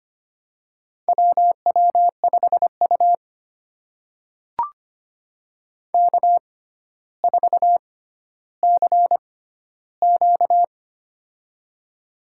Morse code, heard as a keyed tone, about 25 wpm, spaced as for 13 wpm, 700 Hz. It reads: WW5U K4CQ
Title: USA Call Signs - 4 Characters - 6x Character Spacing - 25wpm